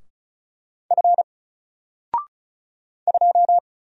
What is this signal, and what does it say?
Morse code, heard as a keyed tone, about 34 wpm, 700 Hz. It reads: F 2
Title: Single Character (Letter:Number:Punctuation:Prosign) 35wpm